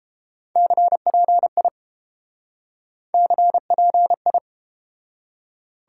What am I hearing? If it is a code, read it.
Morse code, heard as a keyed tone, about 31 wpm, 700 Hz. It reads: CPI CPI